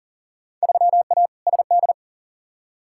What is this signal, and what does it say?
Morse code, heard as a keyed tone, about 40 wpm, 700 Hz. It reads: 3A SD
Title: Field Day Exchanges - 40wpm